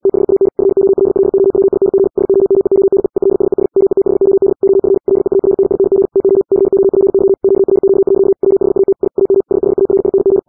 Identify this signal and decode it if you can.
Morse code, heard as a keyed tone, about 80 wpm, 390 Hz. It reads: THAN FOLLOWING SOMEONE ELSES METHOD OF LEARNING TO COPY QRQ. WHAT I AM SHARING